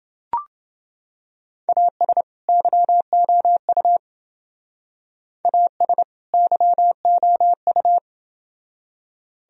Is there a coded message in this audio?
Yes: Morse code from 1.7 to 8.0 s, heard as a keyed tone, about 28 wpm, 700 Hz. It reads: AS YOU AS YOU